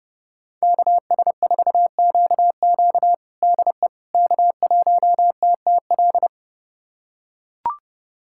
Morse code, heard as a keyed tone, about 30 wpm, 700 Hz. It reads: KS4QQ DE K1TTL
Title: USA Callsign DE Callsign - 5 Characters - 30wpm